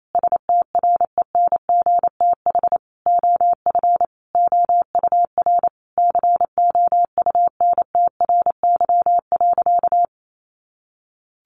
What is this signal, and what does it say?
Morse code, heard as a keyed tone, about 28 wpm, 700 Hz. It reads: STRENGTH OF OUR COUNTRY.
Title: Sentences from Top 1300 Words 28wpm